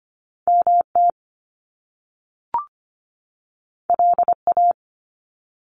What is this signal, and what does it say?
Morse code, heard as a keyed tone, about 25 wpm, 700 Hz. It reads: MT LA
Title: US States 25wpm